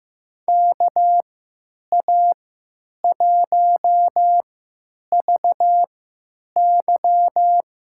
Morse code, heard as a keyed tone, about 15 wpm, spaced as for 9 wpm, 700 Hz. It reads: KA1VY
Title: USA Call Signs - 3x Character Spacing - 15wpm